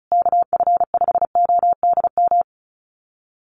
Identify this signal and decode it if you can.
Morse code, heard as a keyed tone, about 34 wpm, 700 Hz. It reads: KF5ODM
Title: USA Call Signs Repeated 3x 35wpm